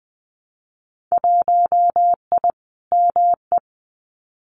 Morse code, heard as a keyed tone, about 20 wpm, 700 Hz. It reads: 1I ME
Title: Winter Field Day Exchanges 20wpm